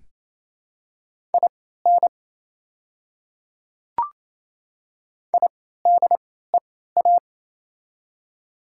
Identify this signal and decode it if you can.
Morse code, heard as a keyed tone, about 27 wpm, spaced as for 16 wpm, 700 Hz. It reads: IN IDEA
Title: 3x Character Spacing - Top 300 Words - 28wpm